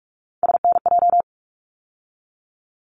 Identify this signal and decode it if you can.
Morse code, heard as a keyed tone, about 44 wpm, 700 Hz. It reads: SNJ
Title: ARRL Sections 45wpm